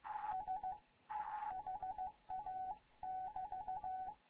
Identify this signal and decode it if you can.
Morse code, heard as a keyed tone, about 15 wpm, 750 Hz, with radio noise. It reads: S HA <BT>
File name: Morze-HA2MN / hangp / 2 11 GYAKORLAT